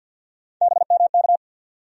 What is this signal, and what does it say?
Morse code, heard as a keyed tone, about 52 wpm, 700 Hz. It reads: BMK
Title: USA Call Sign Postfixes - 3 Letters 50wpm